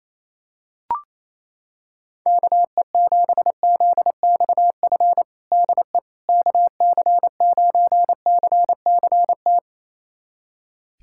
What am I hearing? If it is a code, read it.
Morse code, heard as a keyed tone, about 28 wpm, 700 Hz. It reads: KE7ZXF DE KC9CCT
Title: USA Callsign DE Callsign - 6 Characters - 28wpm